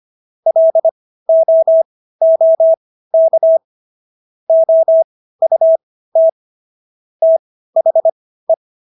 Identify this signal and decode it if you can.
Morse code, heard as a keyed tone, about 25 wpm, spaced as for 15 wpm, 650 Hz. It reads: LOOK OUT THE